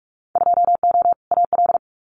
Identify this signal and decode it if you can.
Morse code, heard as a keyed tone, about 46 wpm, 700 Hz. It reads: 2O AL
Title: Winter Field Day Exchanges Encore 45wpm